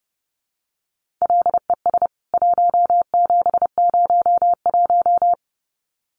Morse code, heard as a keyed tone, about 30 wpm, 700 Hz. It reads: LES 1701
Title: CWT Exchange 30wpm